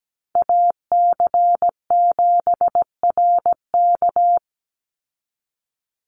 Morse code, heard as a keyed tone, about 17 wpm, 700 Hz. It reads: AC7RK
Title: USA Call Signs 17wpm